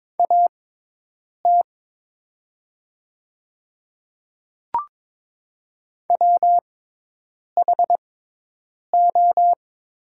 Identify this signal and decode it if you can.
Morse code, heard as a keyed tone, about 22 wpm, spaced as for 8 wpm, 700 Hz. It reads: AT WHO
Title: 6x Character Spacing - Top 100 Words - 22wpm